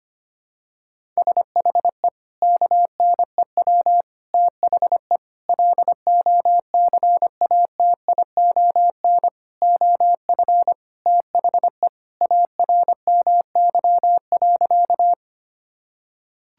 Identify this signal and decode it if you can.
Morse code, heard as a keyed tone, about 25 wpm, 700 Hz. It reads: SHE KNEW THE LOCATION OF THE ARMY.
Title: Sentences from Top 1000 Words - Review Part 6 - 25wpm